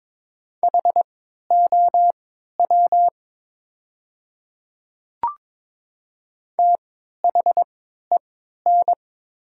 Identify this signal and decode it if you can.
Morse code, heard as a keyed tone, about 22 wpm, spaced as for 12 wpm, 700 Hz. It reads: HOW THEN
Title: 3x Character Spacing - Top 100 Words - 22wpm